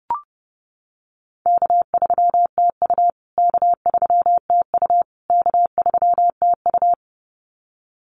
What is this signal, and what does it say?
Morse code, heard as a keyed tone, about 30 wpm, 700 Hz. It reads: K3TU K3TU K3TU
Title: USA Call Signs Repeated 3x 30wpm